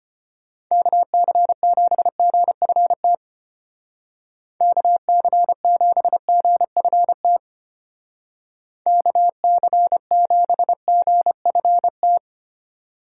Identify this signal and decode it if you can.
Morse code, heard as a keyed tone, about 30 wpm, 700 Hz. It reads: KC7GFT KC7GFT KC7GFT
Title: Speed Racing - USA 2x3 Call Signs 25wpm